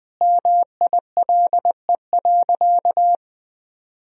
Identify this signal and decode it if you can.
Morse code, heard as a keyed tone, about 21 wpm, 700 Hz. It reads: MILE.
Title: Sentences from Top 1000 Words - Review Part 1 - 20wpm